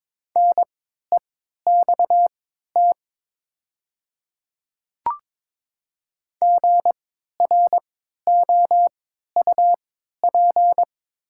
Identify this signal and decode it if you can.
Morse code, heard as a keyed tone, about 22 wpm, spaced as for 12 wpm, 700 Hz. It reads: NEXT GROUP
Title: 3x Character Spacing - Top 300 Words - 22wpm